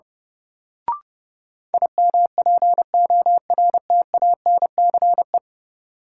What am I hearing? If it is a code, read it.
Morse code, heard as a keyed tone, about 30 wpm, 700 Hz. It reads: IMPORTANCE
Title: Sentences from Top 1200 Words - New Word Drill Alt - 30wpm